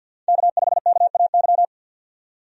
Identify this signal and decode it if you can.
Morse code, heard as a keyed tone, about 53 wpm, 700 Hz. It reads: K5KAY
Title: USA Call Signs - 5 Characters - 50wpm